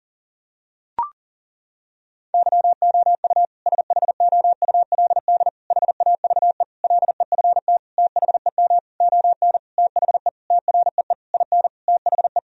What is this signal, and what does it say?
Morse code, heard as a keyed tone, about 37 wpm, 700 Hz. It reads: YOU SHOULD HAVE LEFT THEM ON THE TREE IN THE
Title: Sentences from Top 1000 Words - Review Part 1 - 40wpm